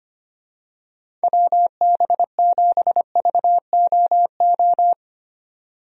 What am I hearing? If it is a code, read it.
Morse code, heard as a keyed tone, about 25 wpm, 700 Hz. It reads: WB7VOO
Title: USA Call Signs - 6 Characters - 25wpm